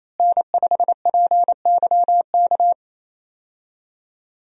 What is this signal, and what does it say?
Morse code, heard as a keyed tone, about 28 wpm, 700 Hz. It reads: N5PYK
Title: USA Call Signs - 5 Characters - Repeated 2x - 28wpm